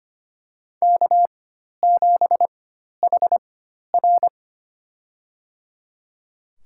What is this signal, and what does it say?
Morse code, heard as a keyed tone, about 25 wpm, spaced as for 12 wpm, 700 Hz. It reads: K7HR
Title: USA Call Signs - 4x Character Spacing - 25wpm